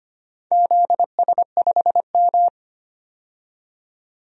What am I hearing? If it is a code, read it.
Morse code, heard as a keyed tone, about 25 wpm, 700 Hz. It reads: ZS5M
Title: DXCC-100 Call Signs - 4 Characters - 25wpm